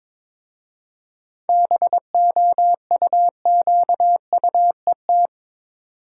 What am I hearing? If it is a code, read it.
Morse code, heard as a keyed tone, about 22 wpm, 690 Hz. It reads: BOUQUET
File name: Difficult Words with Hints - 22wpm